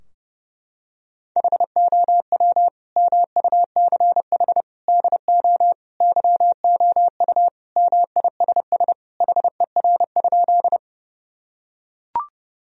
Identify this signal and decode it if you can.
Morse code, heard as a keyed tone, about 32 wpm, 700 Hz. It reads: HOW MUCH DO YOU MISS HER?
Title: Sentences from Top 1000 Words - Review Part 3 - 30wpm